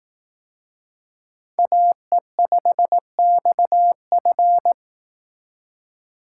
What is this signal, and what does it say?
Morse code, heard as a keyed tone, about 18 wpm, 700 Hz. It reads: AE5XF
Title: Speed Racing - USA Call Signs 15wpm